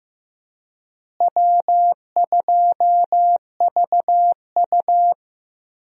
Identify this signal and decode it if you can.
Morse code, heard as a keyed tone, about 15 wpm, 700 Hz. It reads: W2VU
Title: USA Call Signs - 4 Characters - Repeated 2x - 15wpm